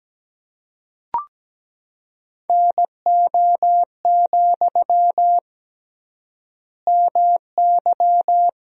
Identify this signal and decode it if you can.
Morse code, heard as a keyed tone, about 17 wpm, 700 Hz. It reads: NO, MY
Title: 3x Word-Spacing - Sentences from Top 400 Words - 17wpm